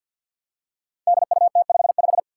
Morse code, heard as a keyed tone, about 53 wpm, 700 Hz. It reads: DUT55
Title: Five Challenging Letter-Number Combinations 50wpm